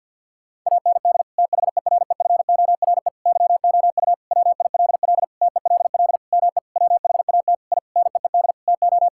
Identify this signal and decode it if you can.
Morse code, heard as a keyed tone, about 50 wpm, 700 Hz. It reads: AND THEREFORE YOU WILL TELL ME WHAT I NEED TO
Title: Sentences from Top 1000 Words - Review Part 5 - 50wpm